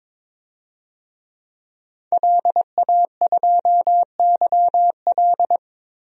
Morse code, heard as a keyed tone, about 22 wpm, 700 Hz. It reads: LA2YL
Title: DXCC-100 Call Signs - 5 Characters - Repeated 2x - 22wpm